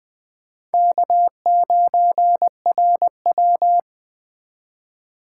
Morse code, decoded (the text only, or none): K9RW